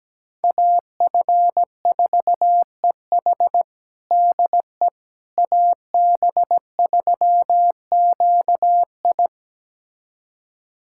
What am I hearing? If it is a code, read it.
Morse code, heard as a keyed tone, about 17 wpm, 700 Hz. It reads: AF4EH DE AB3QI
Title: USA Callsign DE Callsign - 5 Characters - 17wpm